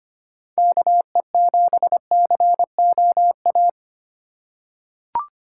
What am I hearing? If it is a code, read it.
Morse code, heard as a keyed tone, about 25 wpm, 700 Hz. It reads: KE7COA
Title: USA Call Signs - 6 Characters - 4x Character Spacing - 25wpm